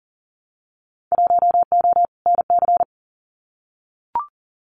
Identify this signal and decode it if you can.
Morse code, heard as a keyed tone, about 41 wpm, 700 Hz. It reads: 1O NC
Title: Winter Field Day Exchanges 40wpm